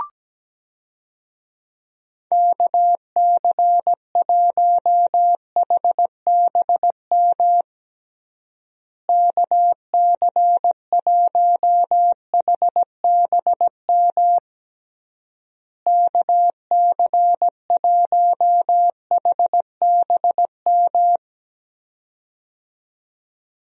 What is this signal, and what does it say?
Morse code, heard as a keyed tone, about 17 wpm, 700 Hz. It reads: KC1HBM KC1HBM KC1HBM
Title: USA Call Signs - 6 Characters - Repeated 3x - 17wpm